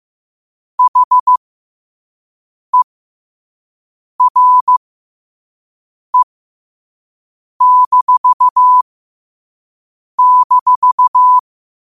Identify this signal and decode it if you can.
Morse code, heard as a keyed tone, about 15 wpm, spaced as for 5 wpm, 1000 Hz. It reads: HERE--
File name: qotdfiles / 5wpm